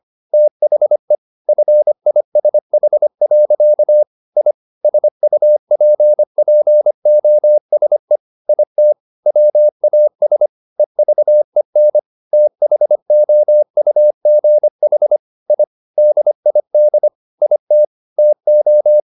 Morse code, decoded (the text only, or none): THE FISH. I SUPPOSE IT WAS EVEN THOUGH I DID IT TO